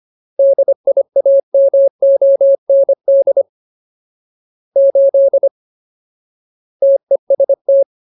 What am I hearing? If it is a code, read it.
Morse code, heard as a keyed tone, about 25 wpm, 550 Hz. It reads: DIAMOND 8 TEST
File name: Ultimate Word List / 25WPM Ultimate